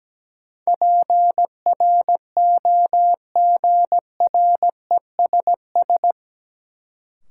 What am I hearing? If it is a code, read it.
Morse code, heard as a keyed tone, about 17 wpm, 700 Hz. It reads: PROGRESS
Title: Sentences from Top 1400 Words - New Word Drill Alt - 17wpm